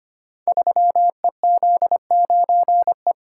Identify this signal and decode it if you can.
Morse code, heard as a keyed tone, about 25 wpm, 700 Hz. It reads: 3EZ9E
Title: Code Groups - 5 Letter:Numbers - 25wpm